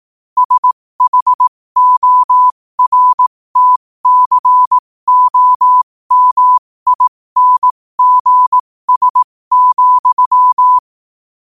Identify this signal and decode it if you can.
Morse code, heard as a keyed tone, about 18 wpm, spaced as for 15 wpm, 1000 Hz. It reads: SHORTCOMINGS,
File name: qotdfiles / 15wpm